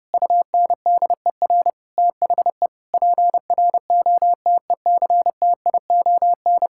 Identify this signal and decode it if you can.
Morse code, heard as a keyed tone, about 32 wpm, 700 Hz. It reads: UNDER THE PROTECTION
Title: Sentences from Top 1600 Words 30wpm